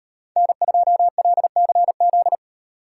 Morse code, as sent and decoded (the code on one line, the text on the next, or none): -. ..--- .-.. -.-. --..
N2LCZ